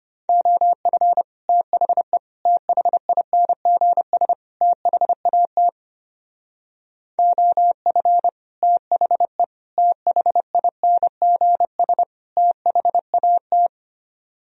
Morse code, heard as a keyed tone, about 27 wpm, 700 Hz. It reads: OF THE THINGS THAT OF THE THINGS THAT
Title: Speed-Racing - Sets of 4 Words 25wpm